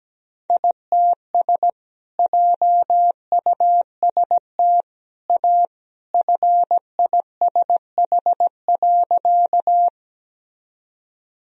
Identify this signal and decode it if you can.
Morse code, heard as a keyed tone, about 17 wpm, 700 Hz. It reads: ITS JUST A FISH.
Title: Sentences from Top 1000 Words - Review Part 1 - 17wpm